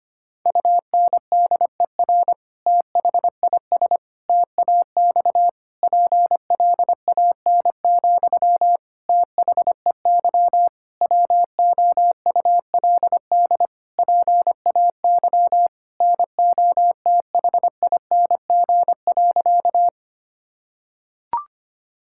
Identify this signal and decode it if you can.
Morse code, heard as a keyed tone, about 25 wpm, 700 Hz. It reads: UNDER THIS TAX PLAN, THEY WOULD PAY NOTHING.
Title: Sentences from Top 1000 Words - Review Part 5 - 25wpm